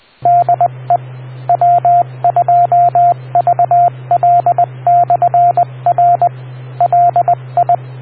Morse code, heard as a keyed tone, about 20 wpm, 700 Hz, with radio noise. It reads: DE W2VL/R LI